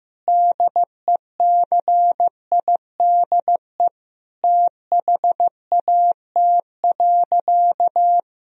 Morse code, heard as a keyed tone, about 15 wpm, 700 Hz. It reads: DECIDE THAT.
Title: Sentences from Top 1000 Words - Review Part 3 - 15wpm